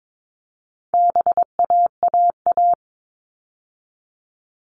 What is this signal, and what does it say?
Morse code, heard as a keyed tone, about 22 wpm, 700 Hz. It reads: BAAA
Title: All Cut Numbers - 4-Digit 22wpm